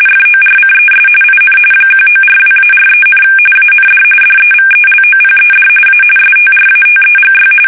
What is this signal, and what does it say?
RTTY, about 45.45 baud. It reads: FORCES IN 1942-43, AND DEVELOPED A STRATEGY
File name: fsk850